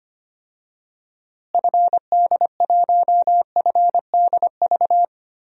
Morse code, heard as a keyed tone, about 25 wpm, 700 Hz. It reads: FD1FDV
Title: DXCC-100 Call Signs - 6 Characters - Repeated 3x - 25wpm